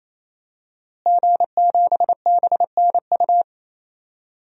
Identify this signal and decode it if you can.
Morse code, heard as a keyed tone, about 27 wpm, 700 Hz. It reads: G7BNU
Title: DXCC-100 Call Signs - 5 Characters - Repeated 3x - 28wpm